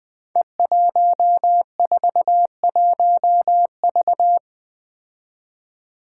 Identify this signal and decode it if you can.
Morse code, heard as a keyed tone, about 21 wpm, 700 Hz. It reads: E141V
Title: Five Challenging Letter-Number Combinations 20wpm